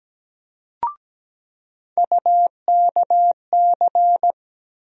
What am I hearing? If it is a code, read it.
Morse code, heard as a keyed tone, about 17 wpm, 700 Hz. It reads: UKC
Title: USA Call Sign Postfixes - 3 Letters 17wpm